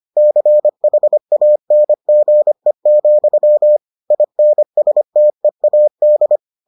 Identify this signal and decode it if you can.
Morse code, heard as a keyed tone, about 25 wpm, 600 Hz. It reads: CHANGE, INSTEAD